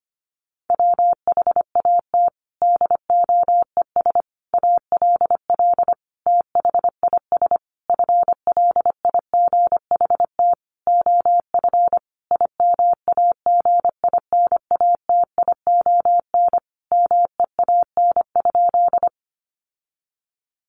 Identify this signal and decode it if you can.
Morse code, heard as a keyed tone, about 25 wpm, 700 Hz. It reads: WHAT DOES ALL THIS FLIGHT OF IMAGINATION MEAN?
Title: Sentences from Top 1300 Words 25wpm